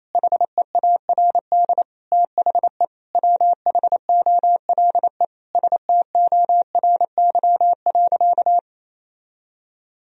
Morse code, heard as a keyed tone, about 28 wpm, 700 Hz. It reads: HEARD THE WHOLE STORY.
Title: Sentences from Top 1000 Words - Review Part 3 - 28wpm